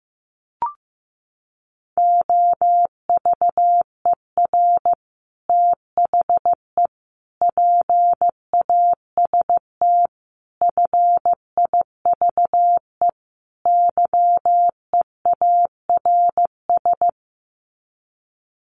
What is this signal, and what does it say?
Morse code, heard as a keyed tone, about 15 wpm, 700 Hz. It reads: OVER THE PAST FIVE YEARS
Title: Sets of 5 Words Encore 15wpm